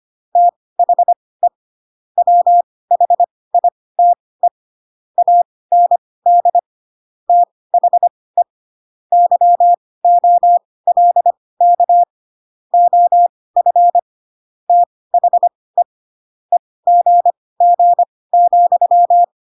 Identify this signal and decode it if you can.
Morse code, heard as a keyed tone, about 25 wpm, spaced as for 18 wpm, 700 Hz. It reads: THE WHITE AND THE YOLK OF THE EGG,